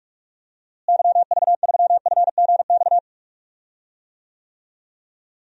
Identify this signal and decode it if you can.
Morse code, heard as a keyed tone, about 41 wpm, 700 Hz. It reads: YV3FGX
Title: DXCC-100 Call Signs - 6 Characters - Repeated 2x - 45wpm